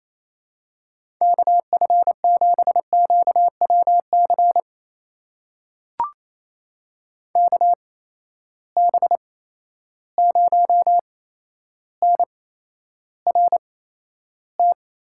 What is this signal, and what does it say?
Morse code, heard as a keyed tone, about 28 wpm, spaced as for 12 wpm, 700 Hz. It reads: KF7QWC KB0NRT
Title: USA Call Signs - 6 Characters - 8x Character Spacing - 28wpm